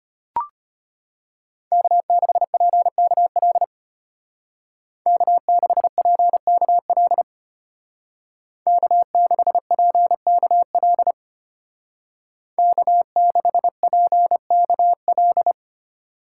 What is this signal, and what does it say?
Morse code, heard as a keyed tone, about 31 wpm, 700 Hz. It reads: K6PKL K6PKL K6PKL K6PKL
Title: Speed Racing - USA 1x3 Call Signs 25wpm